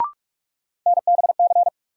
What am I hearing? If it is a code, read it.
Morse code, heard as a keyed tone, about 46 wpm, 700 Hz. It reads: NBC
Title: Sets of 5 Words Encore - New Word Drill - 45wpm